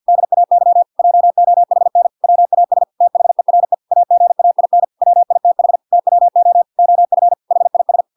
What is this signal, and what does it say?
Morse code, heard as a keyed tone, about 50 wpm, 700 Hz. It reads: DAY JOHN WAS THERE AGAIN WITH TWO OF HIS